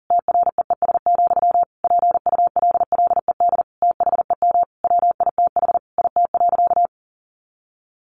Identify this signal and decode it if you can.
Morse code, heard as a keyed tone, about 37 wpm, 700 Hz. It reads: TREES, PULLED THEM WITH IT.
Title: A Sound Of Thunder 40wpm